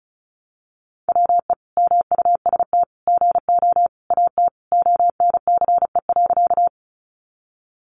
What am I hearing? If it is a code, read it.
Morse code, heard as a keyed tone, about 34 wpm, 700 Hz. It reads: WE MUST GO AT ONCE.
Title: Sentences from Top 300 Words 35wpm